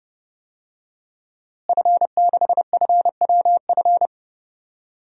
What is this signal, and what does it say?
Morse code, heard as a keyed tone, about 31 wpm, 700 Hz. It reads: F6FWF
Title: DXCC-100 Call Signs - 5 Characters - Repeated 2x - 30wpm